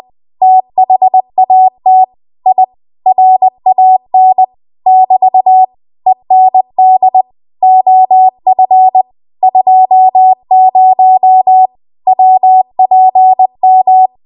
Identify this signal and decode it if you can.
Morse code, heard as a keyed tone, about 20 wpm, 750 Hz. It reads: THAT I RAN <BT> END OF 20 WPM
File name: Archive / 20 WPM / 20WPM